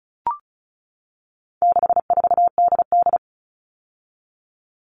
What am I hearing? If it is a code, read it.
Morse code, heard as a keyed tone, about 34 wpm, 700 Hz. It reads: 64DD
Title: All Cut Numbers - 4-Digit 35wpm